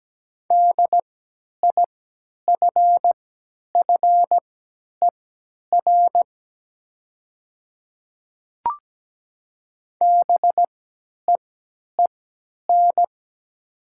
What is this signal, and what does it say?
Morse code, heard as a keyed tone, about 17 wpm, spaced as for 10 wpm, 700 Hz. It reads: DIFFER BEEN
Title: 3x Character Spacing - Top 200 Words - 17wpm